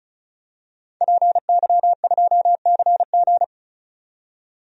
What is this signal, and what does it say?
Morse code, heard as a keyed tone, about 34 wpm, 700 Hz. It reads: PY2CG
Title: DXCC-100 Call Signs - 5 Characters - 35wpm